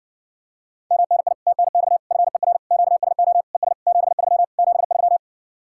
Morse code, heard as a keyed tone, about 54 wpm, 700 Hz. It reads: MNI TNX FER QSO ES 73 73